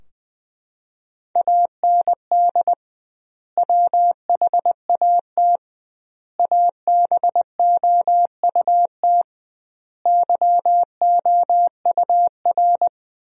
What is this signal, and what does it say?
Morse code, heard as a keyed tone, about 21 wpm, 700 Hz. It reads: AND WHAT ABOUT YOUR
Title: 2x Word-Spacing - Sentences from Top 200 Words - 20wpm